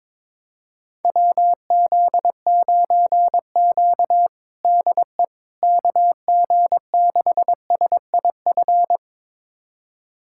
Morse code, heard as a keyed tone, about 22 wpm, 700 Hz. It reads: WZ9Q DE KG6SIF
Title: USA Callsign DE Callsign 22wpm